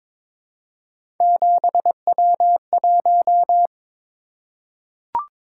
Morse code, heard as a keyed tone, about 22 wpm, 700 Hz. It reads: 7W1
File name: practice / 22wpm / Three Letter-Number Combinations - 22wpm